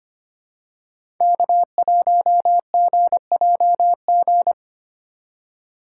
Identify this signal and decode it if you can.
Morse code, heard as a keyed tone, about 25 wpm, 700 Hz. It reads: K1GJG
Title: USA Call Signs - 5 Characters - 2x Character Spacing - 25wpm